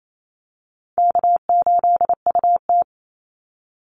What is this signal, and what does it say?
Morse code, heard as a keyed tone, about 28 wpm, 700 Hz. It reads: K8UT
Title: Easy Call Signs Repeated 2x - 28wpm